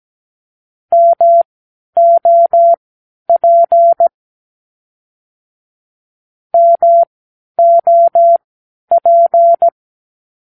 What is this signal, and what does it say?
Morse code, heard as a keyed tone, about 17 wpm, spaced as for 10 wpm, 680 Hz. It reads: MOP MOP
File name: k7qo / Practicewords